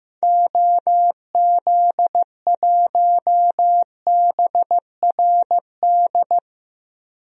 Morse code, heard as a keyed tone, about 15 wpm, 700 Hz. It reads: OZ1BRD
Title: DXCC-100 Call Signs - 6 Characters - Repeated 3x - 15wpm